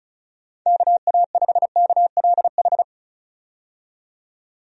Morse code, heard as a keyed tone, about 34 wpm, 700 Hz. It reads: KA5KLH
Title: USA Call Signs - 6 Characters - 35wpm